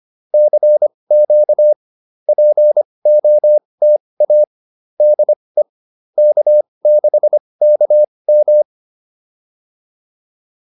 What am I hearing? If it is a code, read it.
Morse code, heard as a keyed tone, about 25 wpm, spaced as for 20 wpm, 600 Hz. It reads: CQ POTA DE K6KM